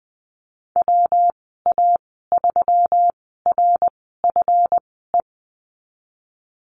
Morse code, heard as a keyed tone, about 20 wpm, spaced as for 15 wpm, 700 Hz. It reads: WA3RFE
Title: 179 Lesson 29g - Slash - Call Signs 20wpm